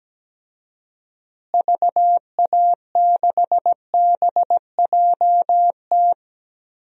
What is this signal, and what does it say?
Morse code, heard as a keyed tone, about 17 wpm, 700 Hz. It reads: VA6BJT
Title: DXCC-100 Call Signs - 6 Characters - Repeated 2x - 17wpm